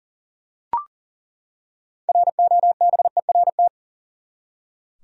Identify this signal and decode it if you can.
Morse code, heard as a keyed tone, about 40 wpm, 700 Hz. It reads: ROBERT
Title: USA Common Names - 40wpm